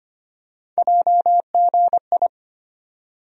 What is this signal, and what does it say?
Morse code, heard as a keyed tone, about 25 wpm, 700 Hz. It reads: JGI
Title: USA Call Sign Postfixes - 3 Letters 25wpm